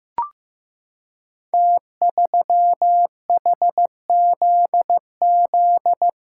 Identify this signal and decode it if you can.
Morse code, heard as a keyed tone, about 15 wpm, 700 Hz. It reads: T3HZZ
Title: Code Groups - 5 Letter:Numbers - 15wpm